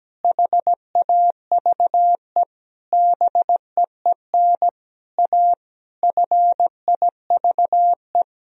Morse code, heard as a keyed tone, about 17 wpm, 700 Hz. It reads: HAVE BEEN A FIVE
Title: Sentences from Top 1000 Words - Review Part 2 - 17wpm